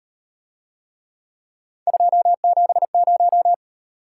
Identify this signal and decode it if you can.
Morse code, heard as a keyed tone, about 38 wpm, 700 Hz. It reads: 270